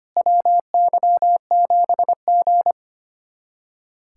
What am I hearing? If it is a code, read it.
Morse code, heard as a keyed tone, about 25 wpm, 700 Hz. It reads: WY7G
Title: USA Callsign DE Callsign - 4 Characters - 25wpm